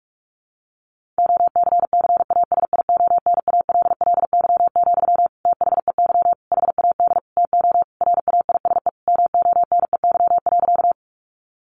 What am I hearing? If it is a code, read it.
Morse code, heard as a keyed tone, about 44 wpm, 700 Hz. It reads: OCCASIONALLY, THEY HAD TO RAISE MONEY.
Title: Sentences from Top 1400 Words 45wpm